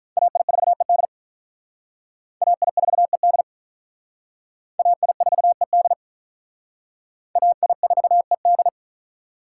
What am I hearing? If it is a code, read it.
Morse code, heard as a keyed tone, about 45 wpm, 700 Hz. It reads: AI4ED AI4ED AI4ED AI4ED